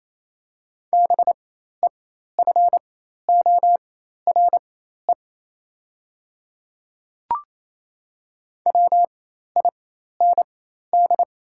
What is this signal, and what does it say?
Morse code, heard as a keyed tone, about 28 wpm, spaced as for 13 wpm, 700 Hz. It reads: BEFORE WIND
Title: 4x Character Spacing - Top 500 Words - 28wpm